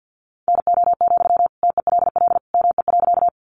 Morse code, heard as a keyed tone, about 50 wpm, 700 Hz. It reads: NO, TELL ME.
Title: Sentences from Top 200 Words 50wpm